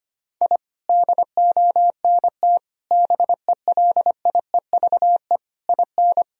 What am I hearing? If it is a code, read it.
Morse code, heard as a keyed tone, about 25 wpm, 700 Hz. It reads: I DONT BELIEVE IN
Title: Sentences from Top 1000 Words - Review Part 2 - 25wpm